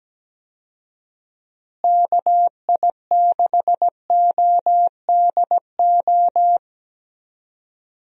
Morse code, heard as a keyed tone, about 17 wpm, 700 Hz. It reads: KI6ODO
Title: USA Call Signs - 6 Characters - Repeated 3x - 17wpm